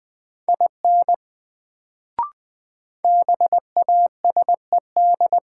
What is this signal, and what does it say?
Morse code, heard as a keyed tone, about 20 wpm, 700 Hz. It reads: IN BASED
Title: Sets of 2 Words - New Word Drill - 20wpm